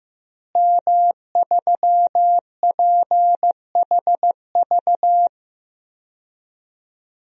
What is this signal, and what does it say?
Morse code, heard as a keyed tone, about 15 wpm, 700 Hz. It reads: M3PHV
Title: DXCC-100 Call Signs - 5 Characters - Repeated 2x - 15wpm